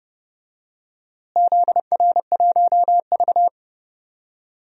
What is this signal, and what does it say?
Morse code, heard as a keyed tone, about 30 wpm, 700 Hz. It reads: ZR1V
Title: DXCC-100 Call Signs - 4 Characters - Repeated 3x - 30wpm